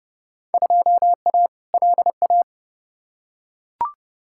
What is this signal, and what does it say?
Morse code, heard as a keyed tone, about 30 wpm, 700 Hz. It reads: 2A LA
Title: Field Day Exchanges Encore - 30wpm